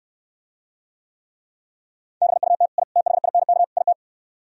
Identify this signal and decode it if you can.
Morse code, heard as a keyed tone, about 67 wpm, 700 Hz. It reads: BUT I THINK IT